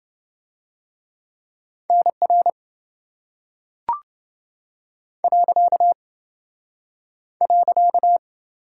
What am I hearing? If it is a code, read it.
Morse code, heard as a keyed tone, about 27 wpm, 700 Hz. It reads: NR .